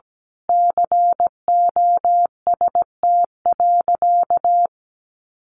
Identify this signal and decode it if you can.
Morse code, heard as a keyed tone, about 17 wpm, 700 Hz. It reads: COST.
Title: Sentences from Top 900 Words 17wpm